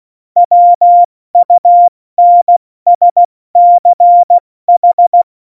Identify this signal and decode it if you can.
Morse code, heard as a keyed tone, about 16 wpm, 700 Hz. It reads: WUNSCH
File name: Morsekurs / Witze / Witz 07. Januar 2020, 70 F80